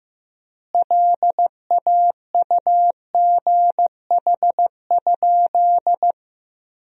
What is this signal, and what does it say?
Morse code, heard as a keyed tone, about 15 wpm, 700 Hz. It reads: LAUGH?
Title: 4x Word-Spacing - Sentences from Top 500 Words - 15wpm